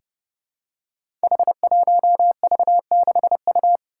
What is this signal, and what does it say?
Morse code, heard as a keyed tone, about 30 wpm, 700 Hz. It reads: H1V6U
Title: Five Challenging Letter-Number Combinations 30wpm